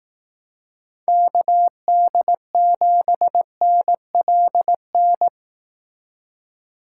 Morse code, heard as a keyed tone, about 18 wpm, 700 Hz. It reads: KD7NLN
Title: Speed Racing - USA 2x3 Call Signs 15wpm